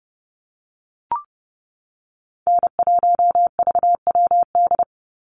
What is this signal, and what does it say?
Morse code, heard as a keyed tone, about 30 wpm, 700 Hz. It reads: N1VWD
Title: NA SOTA Chaser Callsigns 30wpm